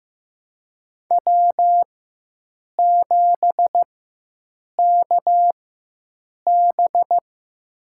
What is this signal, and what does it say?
Morse code, heard as a keyed tone, about 15 wpm, spaced as for 7 wpm, 700 Hz. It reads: W7KB
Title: USA Call Signs - 4 Characters - 4x Character Spacing - 15wpm